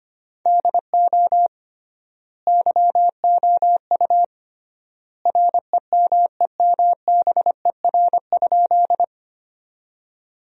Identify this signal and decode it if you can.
Morse code, heard as a keyed tone, about 25 wpm, 700 Hz. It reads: DO YOU REMEMBER?
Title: 3x Word-Spacing - Sentences from Top 400 Words - 25wpm